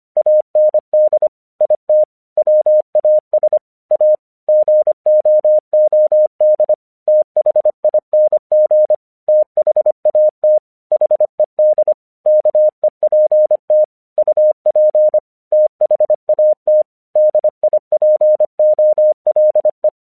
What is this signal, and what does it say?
Morse code, heard as a keyed tone, about 25 wpm, 610 Hz. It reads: AND IT WAS A GOOD THING THAT HED KEPT UP THAT DIPOLE
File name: 1 English 25wpm / AQC26